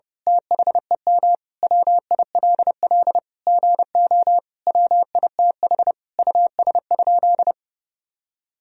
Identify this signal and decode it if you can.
Morse code, heard as a keyed tone, about 30 wpm, 700 Hz. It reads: THEM WILL GO WITH US?
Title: Sentences from Top 1000 Words - Review Part 3 - 30wpm